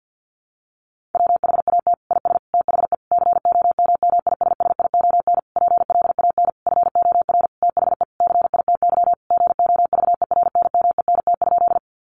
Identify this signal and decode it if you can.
Morse code, heard as a keyed tone, about 50 wpm, 700 Hz. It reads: WHAT IS THE COMMISSION PLAN FOR THE CITY GOVERNMENT?
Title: Sentences from Top 1400 Words 50wpm